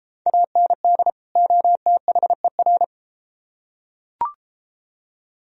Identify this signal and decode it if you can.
Morse code, heard as a keyed tone, about 31 wpm, 700 Hz. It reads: AND OTHER